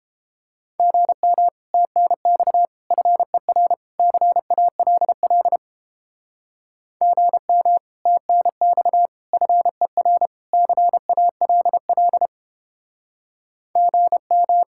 Morse code, heard as a keyed tone, about 31 wpm, 700 Hz. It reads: GM TNX FER CALL GM TNX FER CALL GM